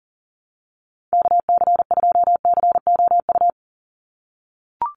Morse code, heard as a keyed tone, about 40 wpm, 700 Hz. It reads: KC2COU
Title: USA Call Signs 40wpm